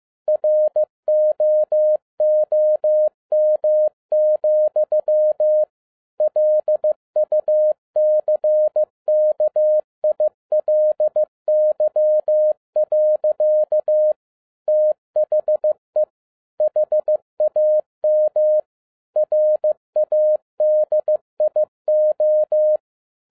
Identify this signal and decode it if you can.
Morse code, heard as a keyed tone, about 15 wpm, 610 Hz. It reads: ROOM, LUCKILY. THE HAM RADIO
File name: 2011-07-mp3 / n12 T8XO 15wpm